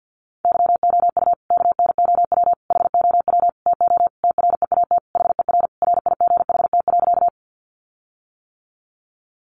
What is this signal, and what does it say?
Morse code, heard as a keyed tone, about 50 wpm, 700 Hz. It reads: YOU KNOW HOW TO TREAT HER RIGHT.
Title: Sentences from Top 1300 Words 50wpm